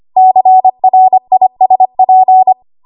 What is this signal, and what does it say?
Morse code, heard as a keyed tone, about 25 wpm, 750 Hz. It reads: CRISP
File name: Archive / 25 WPM / 25WPM